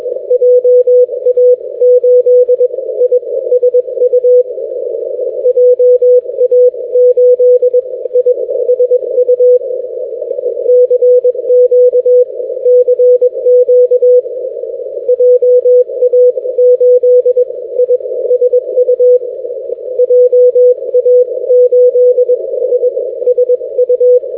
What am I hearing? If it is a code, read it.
Morse code, heard as a keyed tone, about 21 wpm, spaced as for 17 wpm, 500 Hz. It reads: JA8ISU JA8ISU CQ CQ JA8ISU JA8ISU
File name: ja8isu- -x